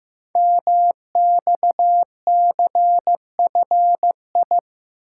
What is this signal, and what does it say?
Morse code, heard as a keyed tone, about 15 wpm, 700 Hz. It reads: MXCFI
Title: Code Groups - 5 Letters - 15wpm